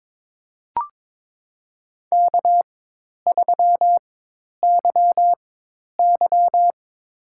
Morse code, heard as a keyed tone, about 22 wpm, spaced as for 10 wpm, 700 Hz. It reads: K3YY
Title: USA Call Signs - 4 Characters - 4x Character Spacing - 22wpm